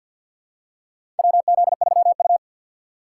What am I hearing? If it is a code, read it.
Morse code, heard as a keyed tone, about 53 wpm, 690 Hz. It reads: WZ3U
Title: USA Call Signs - 2x Character Spacing - 50wpm